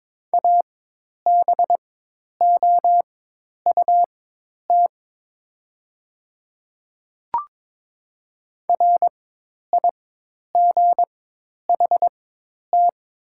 Morse code, heard as a keyed tone, about 22 wpm, spaced as for 10 wpm, 700 Hz. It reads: ABOUT RIGHT
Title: 4x Character Spacing - Top 300 Words - 22wpm